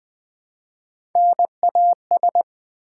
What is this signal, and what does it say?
Morse code, heard as a keyed tone, about 20 wpm, 700 Hz. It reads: NAS
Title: Build A Word - 1 Letter At A Time - 5-Letter Words - 20wpm